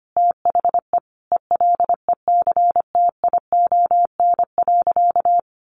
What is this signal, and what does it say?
Morse code, heard as a keyed tone, about 25 wpm, 700 Hz. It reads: THE ELECTION.
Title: Sentences from Top 1500 Words 25wpm